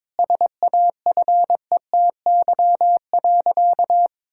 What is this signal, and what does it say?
Morse code, heard as a keyed tone, about 22 wpm, 700 Hz. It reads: SAFETY.
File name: Difficult Sentences with Hints - 22wpm